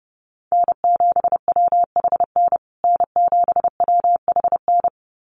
Morse code, heard as a keyed tone, about 30 wpm, 700 Hz. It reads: N7WHN N7WHN
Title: USA Call Signs Repeated 2x 30wpm